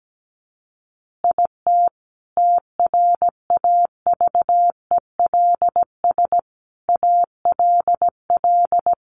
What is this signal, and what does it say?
Morse code, heard as a keyed tone, about 17 wpm, 700 Hz. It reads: IT TRAVELS ALL
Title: Additional Easy Phrases 17wpm